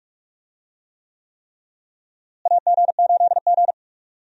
Morse code, heard as a keyed tone, about 46 wpm, 700 Hz. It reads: AG8G
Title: USA Call Signs - 4 Characters - Repeated 3x - 45wpm